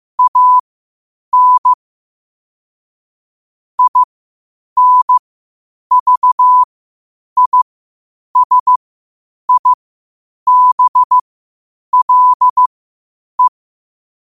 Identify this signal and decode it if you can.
Morse code, heard as a keyed tone, about 15 wpm, spaced as for 8 wpm, 1000 Hz. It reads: AN INVISIBLE
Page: qotd-at-8-wpm